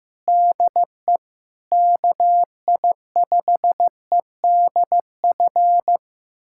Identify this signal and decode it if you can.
Morse code, heard as a keyed tone, about 15 wpm, 700 Hz. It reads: DE KI5EDF
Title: USA Callsign DE Callsign 15wpm